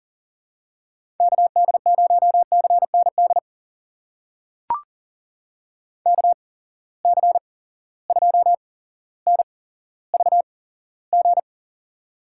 Morse code, heard as a keyed tone, about 40 wpm, spaced as for 17 wpm, 700 Hz. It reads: KD0CND KC2NVG
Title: USA Call Signs - 6 Characters - 8x Character Spacing - 40wpm